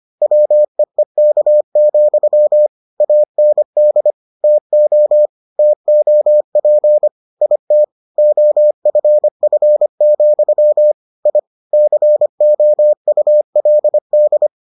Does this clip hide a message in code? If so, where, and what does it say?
Yes: Morse code from 0.2 to 14.5 s, heard as a keyed tone, about 25 wpm, 600 Hz. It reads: WEEK, AND TO TOP IT OFF, I COULD